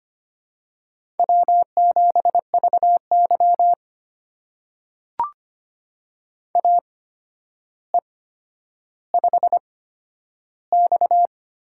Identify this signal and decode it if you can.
Morse code, heard as a keyed tone, about 25 wpm, spaced as for 11 wpm, 700 Hz. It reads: W7VY AE5X
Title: USA Call Signs - 4 Characters - 8x Character Spacing - 25wpm